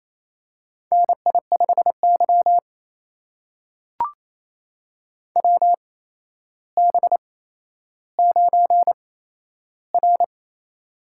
Morse code, heard as a keyed tone, about 27 wpm, spaced as for 12 wpm, 700 Hz. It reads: NI5Y WB9R
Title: USA Call Signs - 8x Character Spacing - 28wpm